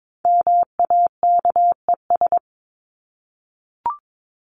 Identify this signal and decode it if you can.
Morse code, heard as a keyed tone, about 22 wpm, 700 Hz. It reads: MAKES
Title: ES:S Suffix - Top 100 Words with Base Words 22wpm